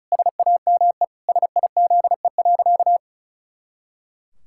Morse code, heard as a keyed tone, about 36 wpm, 700 Hz. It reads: SAME SIZE.
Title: Pangram Challenge v2 - 35wpm